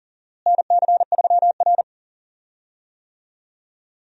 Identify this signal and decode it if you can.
Morse code, heard as a keyed tone, about 37 wpm, 700 Hz. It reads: NC3R
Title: USA Call Signs - 4 Characters - Repeated 2x - 40wpm